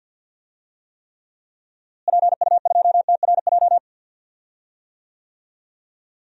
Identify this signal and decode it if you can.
Morse code, heard as a keyed tone, about 48 wpm, 700 Hz. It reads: PU2TRJ
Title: DXCC-100 Call Signs - 6 Characters - Repeated 3x - 50wpm